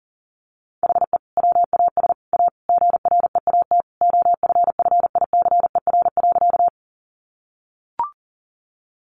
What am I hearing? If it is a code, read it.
Morse code, heard as a keyed tone, about 40 wpm, 700 Hz. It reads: HE WAS A GREAT OFFICER.
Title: Sentences from Top 1200 Words 40wpm